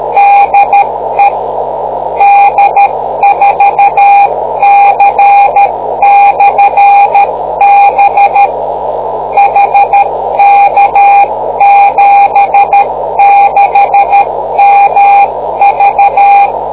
Morse code, heard as a keyed tone, about 13 wpm, 800 Hz, with noise. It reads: DE D4C/B HK76MV